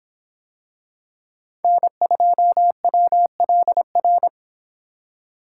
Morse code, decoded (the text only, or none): N2WLR